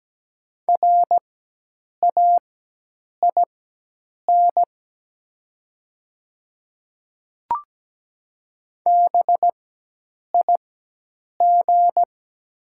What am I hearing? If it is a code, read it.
Morse code, heard as a keyed tone, about 17 wpm, spaced as for 8 wpm, 700 Hz. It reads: RAIN BIG
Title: 4x Character Spacing - Top 500 Words - 17wpm